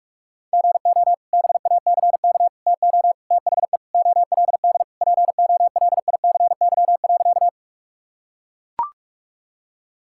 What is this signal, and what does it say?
Morse code, heard as a keyed tone, about 41 wpm, 700 Hz. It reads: GO BACK TO THE OLD POLICY.
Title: Sentences from Top 1000 Words - Review Part 5 - 45wpm